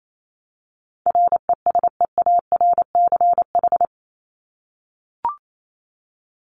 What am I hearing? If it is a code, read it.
Morse code, heard as a keyed tone, about 28 wpm, 700 Hz. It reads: RESEARCH
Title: Top 501-750 Words 28wpm